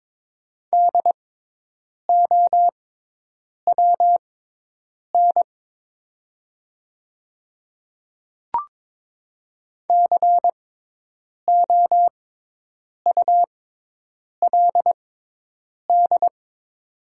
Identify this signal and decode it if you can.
Morse code, heard as a keyed tone, about 22 wpm, spaced as for 8 wpm, 700 Hz. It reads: DOWN COULD
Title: 6x Character Spacing - Top 100 Words - 22wpm